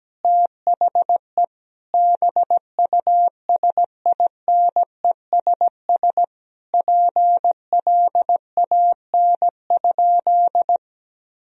Morse code, heard as a keyed tone, about 17 wpm, 700 Hz. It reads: THE BUSINESS PLAN?
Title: Sentences from Top 1000 Words - Review Part 3 - 17wpm